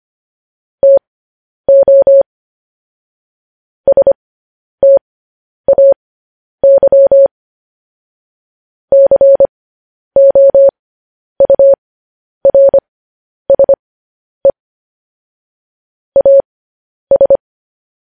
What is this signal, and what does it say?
Morse code, heard as a keyed tone, about 25 wpm, spaced as for 10 wpm, 550 Hz. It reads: TO STAY COURSE AS
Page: news headlines in morse code at 10 wpm